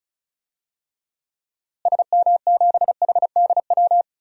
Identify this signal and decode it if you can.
Morse code, heard as a keyed tone, about 34 wpm, 700 Hz. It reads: SM7HDW
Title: DXCC-100 Call Signs - 6 Characters - Repeated 2x - 35wpm